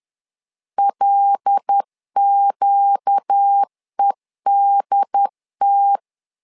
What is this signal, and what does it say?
Morse code, heard as a keyed tone, about 10 wpm, 800 Hz. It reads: LQEDT
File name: LEZIONI / 5 LBVC / 5 L 10wpm 2259